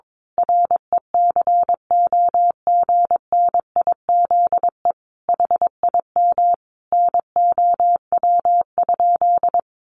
Morse code, heard as a keyed tone, about 22 wpm, 700 Hz. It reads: RECOGNIZE HIM NOW?
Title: Sentences from Top 1400 Words 22wpm